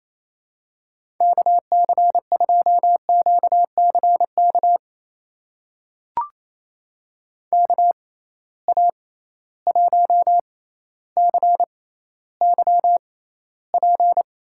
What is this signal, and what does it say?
Morse code, heard as a keyed tone, about 28 wpm, spaced as for 14 wpm, 700 Hz. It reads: KC2QCK KA1CYP
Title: USA Call Signs - 6 Characters - 6x Character Spacing - 28wpm